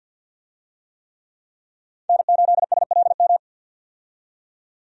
Morse code, heard as a keyed tone, about 51 wpm, 700 Hz. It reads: N8SLM
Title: USA Call Signs - 5 Characters - 50wpm